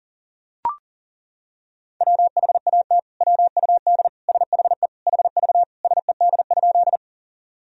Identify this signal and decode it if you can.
Morse code, heard as a keyed tone, about 40 wpm, 700 Hz. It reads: WHAT WUD SHE HV SED?
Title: Sentences from Top 100 Words and Abbreviations 40wpm